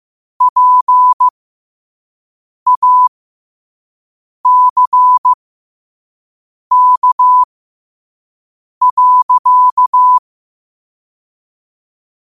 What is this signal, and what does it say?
Morse code, heard as a keyed tone, about 15 wpm, spaced as for 5 wpm, 1000 Hz. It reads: PACK.
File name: qotdfiles / 5wpm